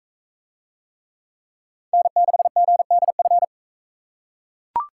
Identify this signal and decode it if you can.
Morse code, heard as a keyed tone, about 42 wpm, 700 Hz. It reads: N6GDF